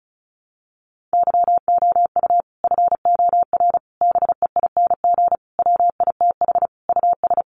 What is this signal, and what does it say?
Morse code, heard as a keyed tone, about 34 wpm, 700 Hz. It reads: YOU FOR BEING WITH US
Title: Sets of 5 Words 35wpm